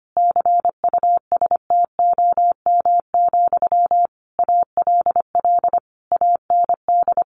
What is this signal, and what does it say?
Morse code, heard as a keyed tone, about 25 wpm, 700 Hz. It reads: CUSTOM, ALL AND
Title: A Sound Of Thunder 25wpm